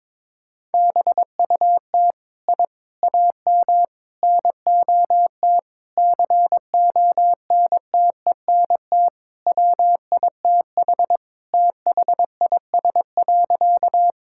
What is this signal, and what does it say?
Morse code, heard as a keyed tone, about 22 wpm, 700 Hz. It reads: BUT I AM NOT CONTENT WITH THIS.
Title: Sentences from Top 1000 Words - Review Part 6 - 22wpm